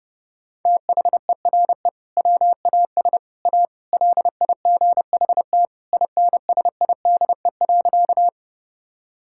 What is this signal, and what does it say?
Morse code, heard as a keyed tone, about 30 wpm, 700 Hz. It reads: THERE WAS A LIGHT INSIDE.
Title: Sentences from Top 1000 Words - Review Part 6 - 30wpm